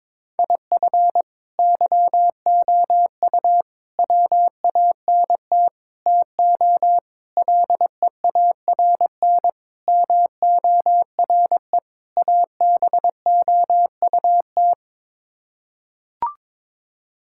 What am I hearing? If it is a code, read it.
Morse code, heard as a keyed tone, about 22 wpm, 700 Hz. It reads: IF YOU WANT TO LEARN MORE ABOUT
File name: practice / 22wpm / Sets of 7 Words - 22wpm